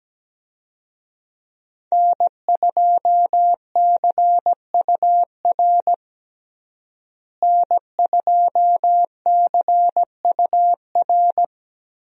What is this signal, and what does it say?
Morse code, heard as a keyed tone, about 17 wpm, 700 Hz. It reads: N2CUR N2CUR
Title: USA Call Signs - 5 Characters - Repeated 3x - 17wpm